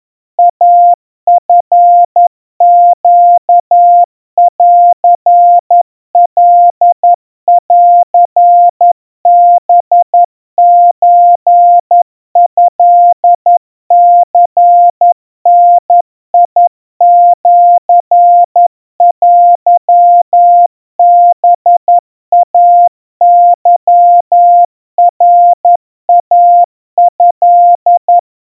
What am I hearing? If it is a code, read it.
Morse code (Japanese Wabun code), heard as a keyed tone, about 11 wpm, 700 Hz. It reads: イチネンカンハソトニダシテハイケナイト
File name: kawa / Morse sample / kamyu 40 cpm 700 Hz Crispy